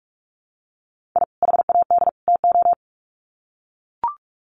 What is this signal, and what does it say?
Morse code, heard as a keyed tone, about 44 wpm, 700 Hz. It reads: I HAD TO
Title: Sets of 3 Words 45wpm